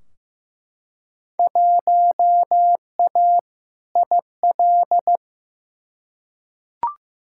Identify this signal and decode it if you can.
Morse code, heard as a keyed tone, about 15 wpm, 700 Hz. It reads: 1A IL
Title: Field Day Exchanges - 15wpm